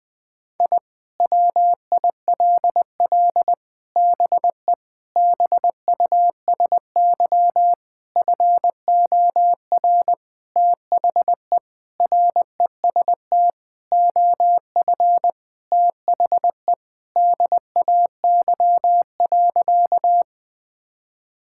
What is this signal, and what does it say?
Morse code, heard as a keyed tone, about 21 wpm, 700 Hz. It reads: I WILL BE BUSY FOR THE REST OF THE DAY.
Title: Sentences with 5-Word Phrases - Spoken Beginning - 20wpm